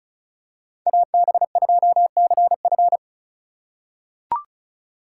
Morse code, heard as a keyed tone, about 34 wpm, 700 Hz. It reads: AB2CF
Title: USA Call Signs - 5 Characters - 4x Character Spacing - 35wpm